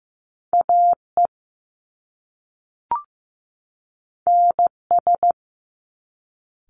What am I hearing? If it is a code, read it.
Morse code, heard as a keyed tone, about 15 wpm, 700 Hz. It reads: AE NS
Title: Top Two Letter Combinations in Words 15wpm